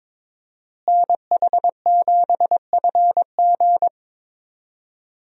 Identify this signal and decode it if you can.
Morse code, heard as a keyed tone, about 22 wpm, 700 Hz. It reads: NH7FG
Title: USA Call Signs - 5 Characters - Repeated 2x - 22wpm